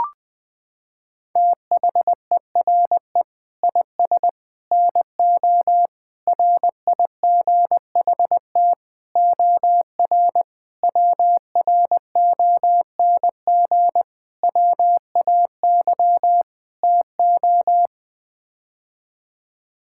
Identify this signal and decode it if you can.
Morse code, heard as a keyed tone, about 21 wpm, 700 Hz. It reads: THERE IS NO RIGHT OR WRONG WAY TO
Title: Sets of 8 Words - 20wpm